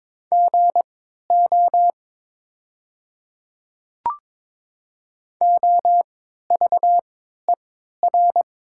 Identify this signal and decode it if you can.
Morse code, heard as a keyed tone, about 22 wpm, spaced as for 13 wpm, 700 Hz. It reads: GO OVER
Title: 3x Character Spacing - Top 200 Words - 22wpm